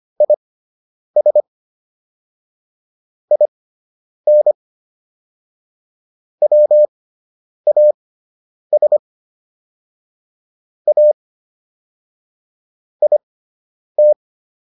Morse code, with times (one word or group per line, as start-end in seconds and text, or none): 0.2-1.4 IS
3.3-4.5 IN
6.4-9.0 WAS
10.9-11.1 A
13.0-14.1 IT